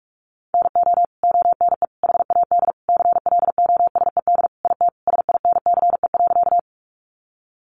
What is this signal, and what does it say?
Morse code, heard as a keyed tone, about 44 wpm, 700 Hz. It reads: NO ONE HAD CLOSED IT SINCE.
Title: Sentences from Top 1400 Words 45wpm